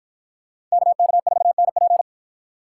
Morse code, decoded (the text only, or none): KK4NP